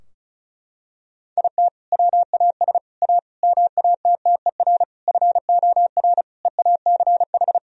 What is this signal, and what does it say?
Morse code, heard as a keyed tone, about 34 wpm, 700 Hz. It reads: IT WAS A MATTER FOR EACH
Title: Sentences from Top 1000 Words - Review Part 4 - 35wpm